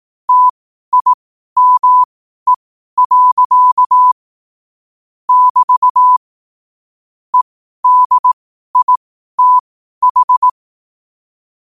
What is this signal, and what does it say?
Morse code, heard as a keyed tone, about 18 wpm, spaced as for 12 wpm, 1000 Hz. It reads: TIME. <BT> EDITH
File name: qotdfiles / 12wpm